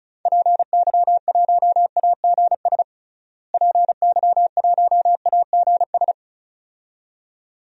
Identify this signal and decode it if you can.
Morse code, heard as a keyed tone, about 36 wpm, 700 Hz. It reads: PY1AGS PY1AGS
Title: DXCC-100 Call Signs - 6 Characters - Repeated 2x - 35wpm